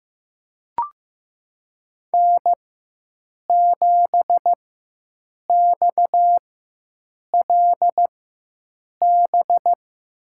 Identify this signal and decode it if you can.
Morse code, heard as a keyed tone, about 15 wpm, spaced as for 7 wpm, 700 Hz. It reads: N7XLB
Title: USA Call Signs - 5 Characters - 4x Character Spacing - 15wpm